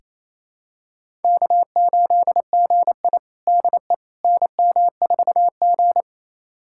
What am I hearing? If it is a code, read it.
Morse code, heard as a keyed tone, about 28 wpm, 700 Hz. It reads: K8GI DE NM4G
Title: USA Callsign DE Callsign - 4 Characters - 28wpm